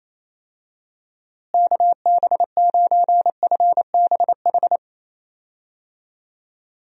Morse code, heard as a keyed tone, about 28 wpm, 700 Hz. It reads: KB9FBH
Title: USA Call Signs - 6 Characters - Repeated 2x - 28wpm